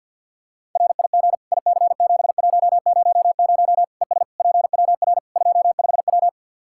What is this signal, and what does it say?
Morse code, heard as a keyed tone, about 50 wpm, 700 Hz. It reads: RIG IC7100 ES PWR 25W
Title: K1USN SST QSO Practice 50wpm